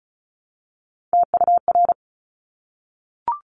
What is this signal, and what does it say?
Morse code, heard as a keyed tone, about 36 wpm, 700 Hz. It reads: TUR
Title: Top Three Letter Combinations in Words 35wpm